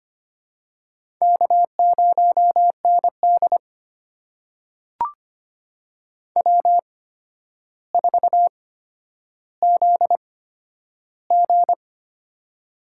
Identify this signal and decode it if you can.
Morse code, heard as a keyed tone, about 25 wpm, spaced as for 11 wpm, 700 Hz. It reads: K0ND W4ZG
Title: USA Call Signs - 4 Characters - 8x Character Spacing - 25wpm